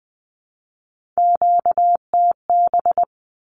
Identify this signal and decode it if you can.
Morse code, heard as a keyed tone, about 20 wpm, 700 Hz. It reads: QTB
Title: Q-Codes 20wpm